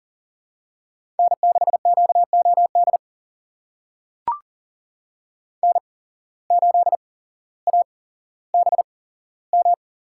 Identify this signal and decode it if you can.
Morse code, heard as a keyed tone, about 40 wpm, spaced as for 17 wpm, 700 Hz. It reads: N6QOD N8ABM
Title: USA Call Signs - 5 Characters - 8x Character Spacing - 40wpm